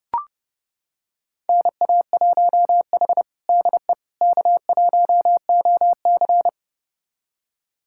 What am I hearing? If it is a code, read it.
Morse code, heard as a keyed tone, about 30 wpm, 700 Hz. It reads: NA1H DE K1OC
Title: USA Callsign DE Callsign - 4 Characters - 30wpm